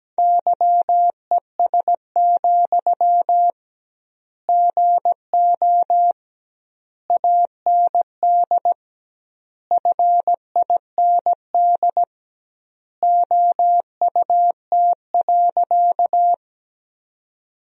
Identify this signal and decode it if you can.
Morse code, heard as a keyed tone, about 17 wpm, 700 Hz. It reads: YES, GO AND FIND OUT.
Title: 2x Word-Spacing - Sentences from Top 500 Words - 17wpm